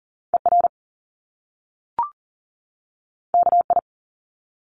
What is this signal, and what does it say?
Morse code, heard as a keyed tone, about 40 wpm, 700 Hz. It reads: ER KI
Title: Top Two Letter Combinations in Words 40wpm